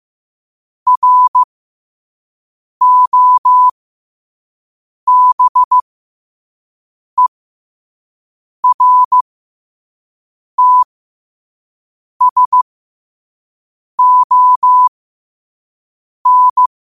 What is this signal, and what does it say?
Morse code, heard as a keyed tone, about 15 wpm, spaced as for 5 wpm, 1000 Hz. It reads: ROBERTSON
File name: qotdfiles / 5wpm